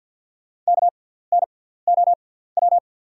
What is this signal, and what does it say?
Morse code, heard as a keyed tone, about 48 wpm, spaced as for 17 wpm, 690 Hz. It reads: KNOW
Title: 6x Character Spacing - Top 300 Words - 50wpm